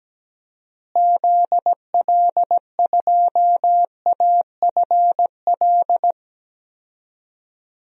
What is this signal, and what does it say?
Morse code, heard as a keyed tone, about 17 wpm, 700 Hz. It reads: ZL2AFL
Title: DXCC-100 Call Signs - 6 Characters - Repeated 2x - 17wpm